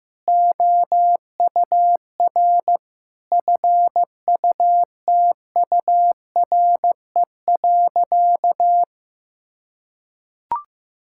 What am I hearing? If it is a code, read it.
Morse code, heard as a keyed tone, about 15 wpm, 700 Hz. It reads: OUR FUTURE.
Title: Sentences from Top 1600 Words 15wpm